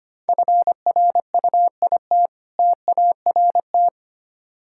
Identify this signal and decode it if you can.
Morse code, heard as a keyed tone, about 25 wpm, 700 Hz. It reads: FRUIT TART
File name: practice / 25wpm / Difficult Words with Hints - 25wpm